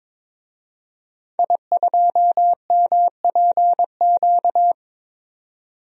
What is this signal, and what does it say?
Morse code, heard as a keyed tone, about 22 wpm, 700 Hz. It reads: I2MPQ
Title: DXCC-100 Call Signs - 5 Characters - 22wpm